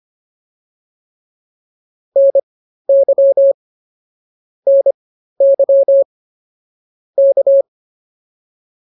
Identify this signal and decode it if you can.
Morse code, heard as a keyed tone, about 25 wpm, spaced as for 13 wpm, 550 Hz. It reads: NY NY K